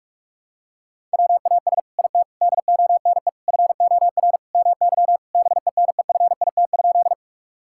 Morse code, heard as a keyed tone, about 46 wpm, 700 Hz. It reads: WAS IT DONE FOR MY BENEFIT?
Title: Sentences from Top 1000 Words - Review Part 6 - 45wpm